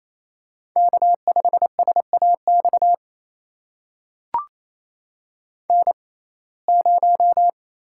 Morse code, heard as a keyed tone, about 28 wpm, spaced as for 20 wpm, 700 Hz. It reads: K5SAX N0
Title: USA Call Signs - 5 Characters - 6x Character Spacing - 28wpm